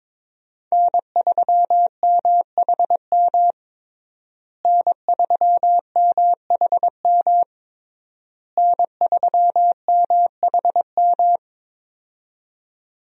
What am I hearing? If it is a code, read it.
Morse code, heard as a keyed tone, about 22 wpm, 700 Hz. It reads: N3MHM N3MHM N3MHM
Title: USA Call Signs - 5 Characters - Repeated 3x - 22wpm